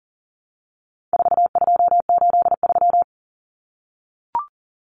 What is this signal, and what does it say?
Morse code, heard as a keyed tone, about 40 wpm, 700 Hz. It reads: 4283
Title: Common Cut Numbers - 4-Digit 40wpm